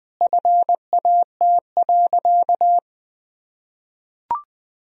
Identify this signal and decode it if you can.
Morse code, heard as a keyed tone, about 20 wpm, 700 Hz. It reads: FAT.
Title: Sentences from Top 1000 Words - Review Part 4 - 20wpm